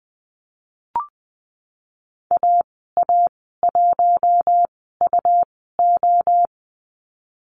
Morse code, heard as a keyed tone, about 20 wpm, spaced as for 14 wpm, 700 Hz. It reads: AA1UO
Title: 122 Lesson 20g - Number 3 - Call Signs 20wpm